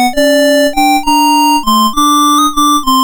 There is no code.